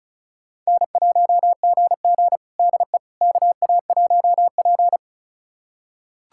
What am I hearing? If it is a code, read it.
Morse code, heard as a keyed tone, about 36 wpm, 700 Hz. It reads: N1GG DE KA1P
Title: USA Callsign DE Callsign - 4 Characters - 35wpm